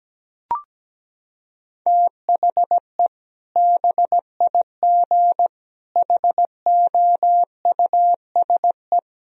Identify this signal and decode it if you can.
Morse code, heard as a keyed tone, about 17 wpm, 700 Hz. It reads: THE BIG HOUSE
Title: Sentences from Top 1000 Words - Review Part 1 - 17wpm